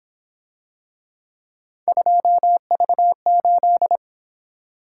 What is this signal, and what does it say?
Morse code, heard as a keyed tone, about 26 wpm, 700 Hz. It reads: 2V8
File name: Speed-Racing - Three Letter-Number Combinations - 22wpm